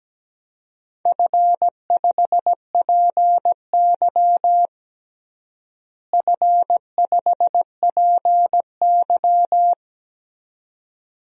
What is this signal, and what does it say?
Morse code, heard as a keyed tone, about 17 wpm, 700 Hz. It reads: F5PY F5PY
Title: DXCC-100 Call Signs - 4 Characters - Repeated 2x - 17wpm